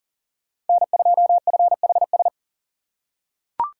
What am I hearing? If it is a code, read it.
Morse code, heard as a keyed tone, about 39 wpm, 700 Hz. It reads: N2FHS
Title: USA Call Signs - 2x Character Spacing - 40wpm